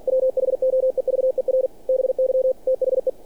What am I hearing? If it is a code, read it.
Morse code, heard as a keyed tone, about 48 wpm, 520 Hz. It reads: WHOEVER BY THE
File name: archived / CRSnet